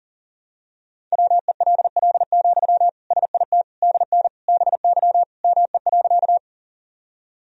Morse code, heard as a keyed tone, about 40 wpm, 700 Hz. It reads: WELL, SIT DN BY ME.
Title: Sentences from Top 500 Words and Abbreviations 40wpm